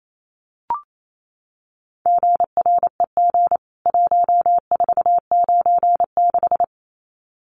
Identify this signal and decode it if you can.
Morse code, heard as a keyed tone, about 28 wpm, 700 Hz. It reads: GREG 1496
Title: CWT Exchange 28wpm